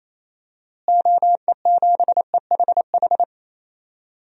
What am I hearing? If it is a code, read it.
Morse code, heard as a keyed tone, about 28 wpm, 700 Hz. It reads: OE7EHH
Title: DXCC-100 Call Signs - 6 Characters - Repeated 3x - 28wpm